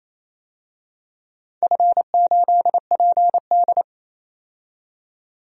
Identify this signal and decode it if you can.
Morse code, heard as a keyed tone, about 28 wpm, 700 Hz. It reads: F8PD
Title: DXCC-100 Call Signs - 4 Characters - 28wpm